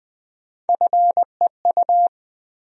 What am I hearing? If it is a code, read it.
Morse code, heard as a keyed tone, about 20 wpm, 700 Hz. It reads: FEU
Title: USA Call Sign Postfixes - 3 Letters 20wpm